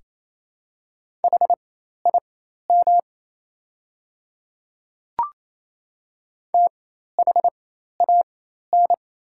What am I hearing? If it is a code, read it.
Morse code, heard as a keyed tone, about 28 wpm, spaced as for 13 wpm, 700 Hz. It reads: HIM THAN
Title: 4x Character Spacing - Top 100 Words - 28wpm